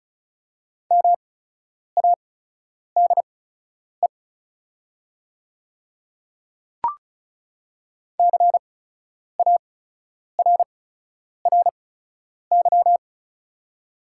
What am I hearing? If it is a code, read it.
Morse code, heard as a keyed tone, about 36 wpm, spaced as for 10 wpm, 690 Hz. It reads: MADE CARRY
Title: 8x Character Spacing - Top 300 Words - 35wpm